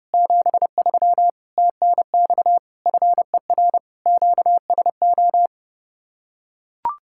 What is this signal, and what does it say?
Morse code, heard as a keyed tone, about 32 wpm, 700 Hz. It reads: 73 TNX FER QSO